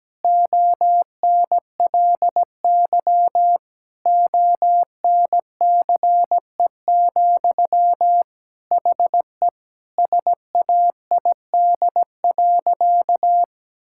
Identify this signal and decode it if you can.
Morse code, heard as a keyed tone, about 17 wpm, 700 Hz. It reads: ONLY ONCE, HE SAID.
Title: Sentences from Top 1600 Words 17wpm